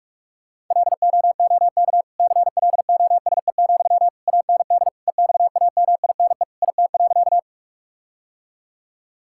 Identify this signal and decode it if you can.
Morse code, heard as a keyed tone, about 46 wpm, 700 Hz. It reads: LOOK CLOSE, AND EXAMINE IT.
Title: Sentences from Top 1600 Words 45wpm